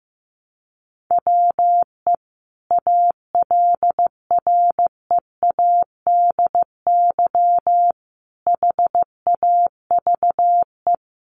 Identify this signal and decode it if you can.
Morse code, heard as a keyed tone, about 15 wpm, 700 Hz. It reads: WE ALREADY HAVE
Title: Sentences from Top 700 Words 15wpm